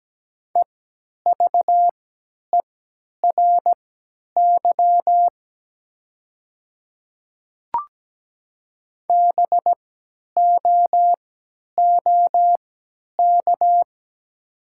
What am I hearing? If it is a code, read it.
Morse code, heard as a keyed tone, about 17 wpm, spaced as for 10 wpm, 700 Hz. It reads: EVERY BOOK
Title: 3x Character Spacing - Top 300 Words - 17wpm